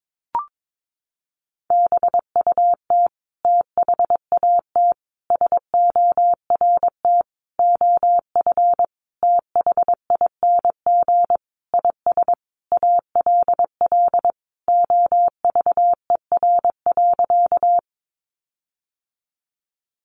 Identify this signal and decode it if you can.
Morse code, heard as a keyed tone, about 22 wpm, 700 Hz. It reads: BUT THAT SORT OF THING IS ALL OVER.
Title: Sentences from Top 1300 Words 22wpm